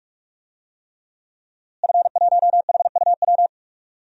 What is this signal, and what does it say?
Morse code, heard as a keyed tone, about 46 wpm, 700 Hz. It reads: F1HUW
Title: DXCC-100 Call Signs - 5 Characters - Repeated 3x - 45wpm